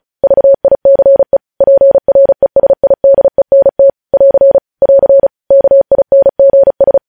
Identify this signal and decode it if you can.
Morse code, heard as a keyed tone, about 36 wpm, 550 Hz. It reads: VICE PRESIDENT <AR> <AR> KINGS
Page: News headlines in Morse code at 35 wpm